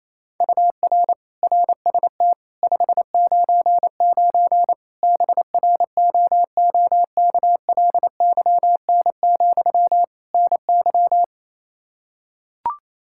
Morse code, heard as a keyed tone, about 28 wpm, 700 Hz. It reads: UR RST 599 BROOKLYN, NY
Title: Practice QSO - Exchange 1 - Beginning - 28wpm